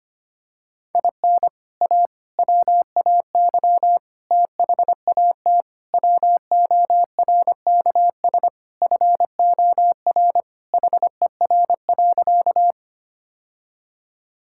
Morse code, heard as a keyed tone, about 25 wpm, 700 Hz. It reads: IN A WAY THAT WORKS FOR HER.
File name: practice / 25wpm / Sentences with 4-Word Phrases - 25wpm